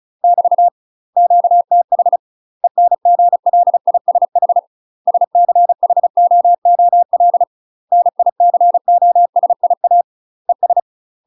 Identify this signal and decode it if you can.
Morse code, heard as a keyed tone, about 35 wpm, 700 Hz. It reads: <BT> QTH ENGLISH SCHOOL NICOSIA ES
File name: UK MASTER QSOs TRAINING/35WPM